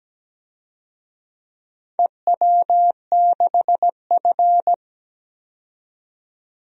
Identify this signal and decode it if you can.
Morse code, heard as a keyed tone, about 17 wpm, 700 Hz. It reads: EW6F
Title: DXCC-100 Call Signs - 4 Characters - 17wpm